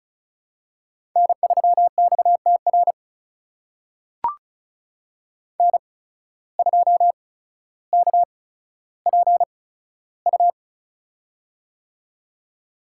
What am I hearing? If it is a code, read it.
Morse code, heard as a keyed tone, about 36 wpm, spaced as for 15 wpm, 700 Hz. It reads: N3XTR N2KPU
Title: USA Call Signs - 5 Characters - 8x Character Spacing - 35wpm